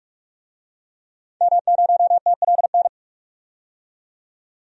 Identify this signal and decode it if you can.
Morse code, heard as a keyed tone, about 47 wpm, 700 Hz. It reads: M0TLN
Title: DXCC-100 Call Signs - 5 Characters - 45wpm